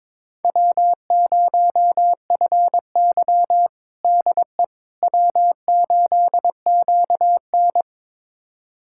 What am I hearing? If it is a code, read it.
Morse code, heard as a keyed tone, about 22 wpm, 700 Hz. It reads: W0FY DE W8QN
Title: USA Callsign DE Callsign - 4 Characters - 22wpm